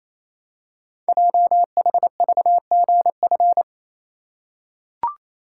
Morse code, heard as a keyed tone, about 27 wpm, 700 Hz. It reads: JHVGF
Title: Code Groups - 5 Letters - 28wpm